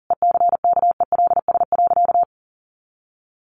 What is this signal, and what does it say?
Morse code, heard as a keyed tone, about 40 wpm, 700 Hz. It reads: ECKELS.
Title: A Sound Of Thunder 40wpm